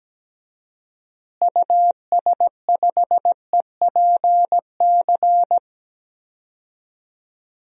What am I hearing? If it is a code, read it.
Morse code, heard as a keyed tone, about 17 wpm, 700 Hz. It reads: US5EPC
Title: DXCC-100 Call Signs - 6 Characters - Repeated 3x - 17wpm